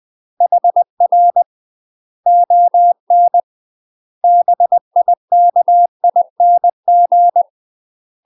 Morse code, heard as a keyed tone, about 20 wpm, 700 Hz. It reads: HR ON BIKING
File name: UK MASTER QSOs TRAINING/20WPM